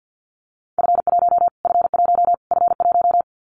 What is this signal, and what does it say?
Morse code, heard as a keyed tone, about 50 wpm, 700 Hz. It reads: F1 F1 F1
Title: Callsign Prefixes for Top 25 DXCC - Repeated 3x - 50wpm